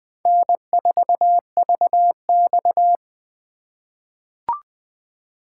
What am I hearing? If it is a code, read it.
Morse code, heard as a keyed tone, about 20 wpm, 700 Hz. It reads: N4VX
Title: USA Call Signs - 4 Characters - 6x Character Spacing - 20wpm